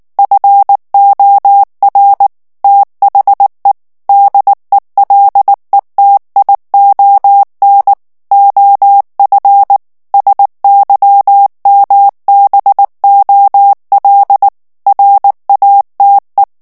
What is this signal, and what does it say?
Morse code, heard as a keyed tone, about 19 wpm, 800 Hz. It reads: FOR THE DELETION OF SYMBOL RATE
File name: ANGLAIS texte clair 3 vitesse 18 mots minute